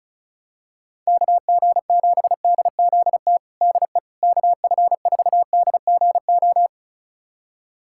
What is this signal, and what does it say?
Morse code, heard as a keyed tone, about 36 wpm, 700 Hz. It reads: KG7DZT DE KF4DGO
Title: USA Callsign DE Callsign - 6 Characters - 35wpm